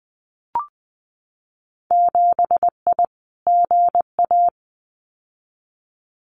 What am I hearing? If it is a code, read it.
Morse code, heard as a keyed tone, about 20 wpm, 700 Hz. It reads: 7I GA
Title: Winter Field Day Exchanges Encore 20wpm